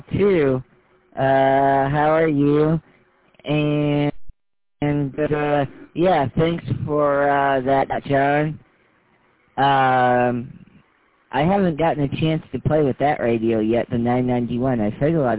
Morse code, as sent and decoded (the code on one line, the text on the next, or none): none